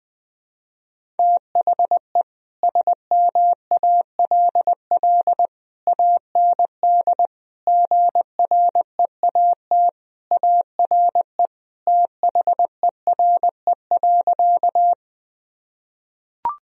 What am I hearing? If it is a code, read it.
Morse code, heard as a keyed tone, about 21 wpm, 700 Hz. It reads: THE SMALL AND GREAT ARE THERE.
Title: Sentences from Top 1000 Words - Review Part 1 - 20wpm